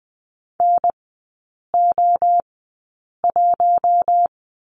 Morse code, heard as a keyed tone, about 20 wpm, spaced as for 8 wpm, 700 Hz. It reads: NO1
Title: 074 Lesson 13e - Letter L - Three Characters 20wpm